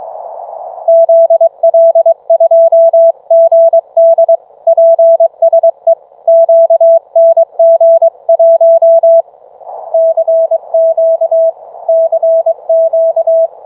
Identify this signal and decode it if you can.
Morse code, heard as a keyed tone, about 27 wpm, 660 Hz. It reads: ZL2GD PSE QNG1 CQ CQ